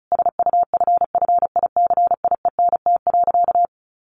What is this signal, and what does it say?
Morse code, heard as a keyed tone, about 34 wpm, 700 Hz. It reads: SUFFICIENT.
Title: Sentences from Top 1400 Words 35wpm